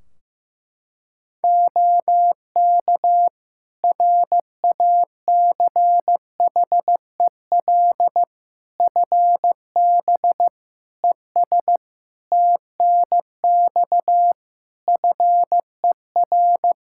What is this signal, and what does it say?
Morse code, heard as a keyed tone, about 15 wpm, 700 Hz. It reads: OK RACHEL FB ES TNX FER
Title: Practice QSO - Exchange 2 - Info - 15wpm